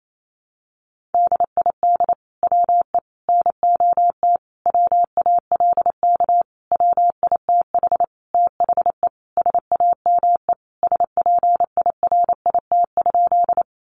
Morse code, heard as a keyed tone, about 28 wpm, 700 Hz. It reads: DID WE NOT WALK WITH THE SAME SPIRIT?
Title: Sentences from Top 1400 Words 28wpm